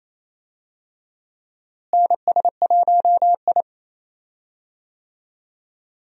Morse code, heard as a keyed tone, about 27 wpm, 700 Hz. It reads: NS1I
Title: USA Call Signs - 4 Characters - Repeated 3x - 28wpm